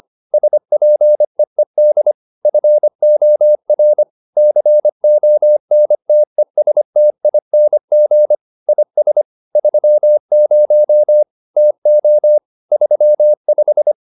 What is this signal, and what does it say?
Morse code, heard as a keyed tone, about 25 wpm, 600 Hz. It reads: SPEED FOR CONTESTING IS 30 TO 35